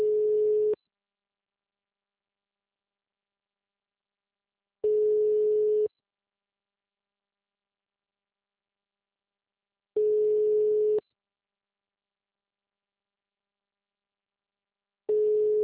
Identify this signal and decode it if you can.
Ringback tone (Europe, 425 Hz).
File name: out- -2002- - - .235